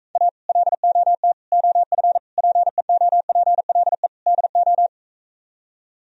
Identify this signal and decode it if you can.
Morse code, heard as a keyed tone, about 45 wpm, 700 Hz. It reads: A LOT OF PEOPLE DO